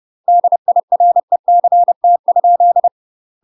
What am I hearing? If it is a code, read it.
Morse code, heard as a keyed tone, about 30 wpm, 700 Hz. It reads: DIRECT?
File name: UK MASTER QSOs TRAINING/30WPM